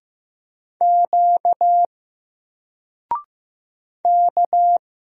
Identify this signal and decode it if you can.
Morse code, heard as a keyed tone, about 15 wpm, 700 Hz. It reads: Q K
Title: Single Letter-Number Alt 15wpm